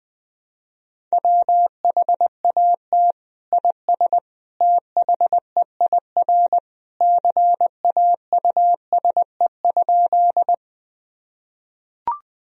Morse code, heard as a keyed tone, about 20 wpm, 700 Hz. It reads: WHAT IS THEIR CAUSE?
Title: Sentences from Top 1000 Words - Review Part 1 - 20wpm